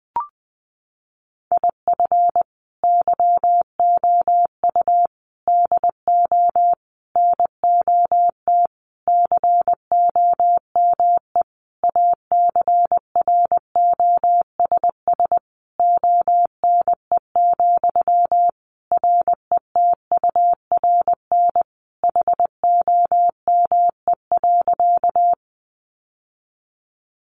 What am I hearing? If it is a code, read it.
Morse code, heard as a keyed tone, about 19 wpm, 700 Hz. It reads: IF YOU DO NOT COME ACROSS ONE, RETURN HOME.
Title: Sentences from Top 800 Words 20wpm